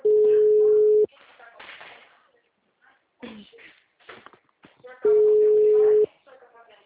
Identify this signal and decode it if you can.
Ringback tone (Europe, 425 Hz), with speech.